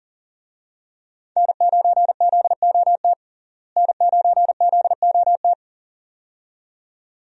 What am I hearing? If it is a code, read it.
Morse code, heard as a keyed tone, about 37 wpm, 700 Hz. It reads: N9ZOT N9ZOT
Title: USA Call Signs - 5 Characters - Repeated 2x - 40wpm